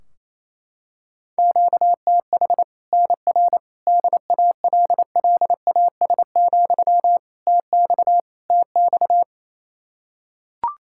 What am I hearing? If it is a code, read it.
Morse code, heard as a keyed tone, about 28 wpm, 700 Hz. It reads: QTH NR DALLAS, TX TX
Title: Practice QSO - Exchange 1 - Beginning - 28wpm